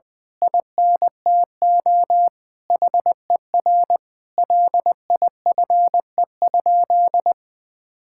Morse code, heard as a keyed tone, about 20 wpm, 700 Hz. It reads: INTO HER LIFE?
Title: Sentences from Top 1000 Words - Review Part 6 - 20wpm